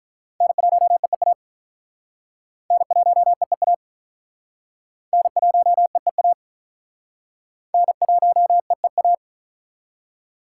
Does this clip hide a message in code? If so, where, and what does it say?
Yes: Morse code from 0.4 to 9.1 s, heard as a keyed tone, about 47 wpm, 700 Hz. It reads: N1EEA N1EEA N1EEA N1EEA